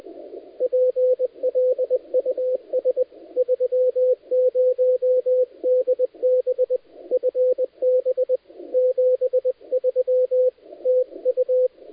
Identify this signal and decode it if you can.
Morse code, heard as a keyed tone, about 20 wpm, 510 Hz, with radio noise. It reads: PLUS 30DB FB 73 TU
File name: dlazaawansowanych / nagraniazpasma / titawaNETcontrol